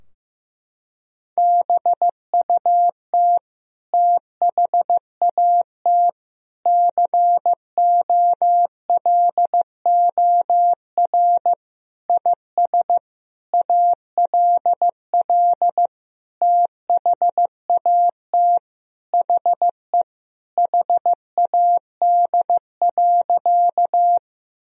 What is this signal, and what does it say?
Morse code, heard as a keyed tone, about 15 wpm, 700 Hz. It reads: BUT THAT COLOR IS ALL THAT HE HAD.
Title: Sentences from Top 1000 Words - Review Part 2 - 15wpm